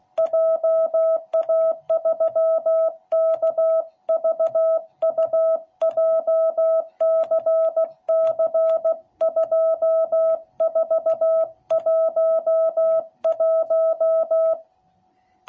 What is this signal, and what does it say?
Morse code, heard as a keyed tone, about 16 wpm, 660 Hz. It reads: JA3KVU JCC2411